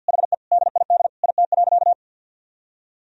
Morse code, heard as a keyed tone, about 50 wpm, 700 Hz. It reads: HE DID IT.
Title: Sentences from Top 1000 Words - Review Part 4 - 50wpm